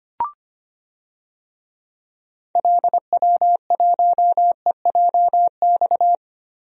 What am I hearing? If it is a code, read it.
Morse code, heard as a keyed tone, about 25 wpm, 700 Hz. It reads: LW1EJX
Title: DXCC-100 Call Signs - 6 Characters - Repeated 2x - 25wpm